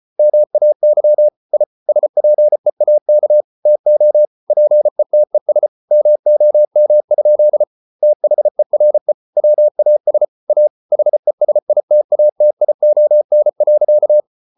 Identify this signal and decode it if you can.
Morse code, heard as a keyed tone, about 33 wpm, 600 Hz. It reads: MAY I SPEAK TO PETES MOM? THERE WAS A HESITATION.